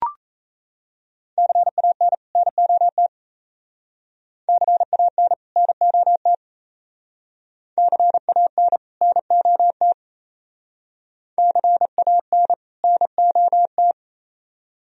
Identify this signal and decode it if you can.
Morse code, heard as a keyed tone, about 35 wpm, 700 Hz. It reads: CAN NOT CAN NOT CAN NOT CAN NOT
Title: Speed-Racing - Sets of 2 Words 28wpm